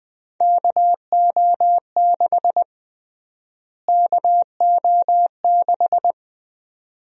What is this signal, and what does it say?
Morse code, heard as a keyed tone, about 20 wpm, 700 Hz. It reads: KO6 KO6
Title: USA Call Sign Prefixes - Repeated 2x - 20wpm